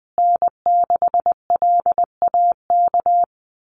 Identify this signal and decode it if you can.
Morse code, heard as a keyed tone, about 20 wpm, 700 Hz. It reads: N6LAK
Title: USA Call Signs Repeated 2x 20wpm